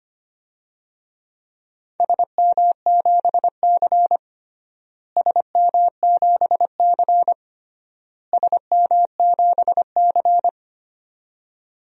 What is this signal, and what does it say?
Morse code, heard as a keyed tone, about 25 wpm, 700 Hz. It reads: SM7C SM7C SM7C
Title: DXCC-100 Call Signs - 4 Characters - Repeated 3x - 25wpm